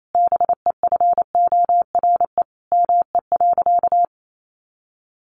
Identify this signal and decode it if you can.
Morse code, heard as a keyed tone, about 28 wpm, 700 Hz. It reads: BEFORE ME.
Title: Sentences from Top 1300 Words 28wpm